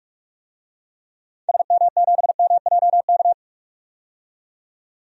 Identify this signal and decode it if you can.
Morse code, heard as a keyed tone, about 46 wpm, 700 Hz. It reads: SM7MJK
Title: DXCC-100 Call Signs - 6 Characters - 45wpm